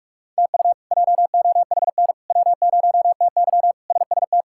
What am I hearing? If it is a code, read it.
Morse code, heard as a keyed tone, about 46 wpm, 700 Hz. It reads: TU JOHN W0TY SST
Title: K1USN SST QSO Practice - 45wpm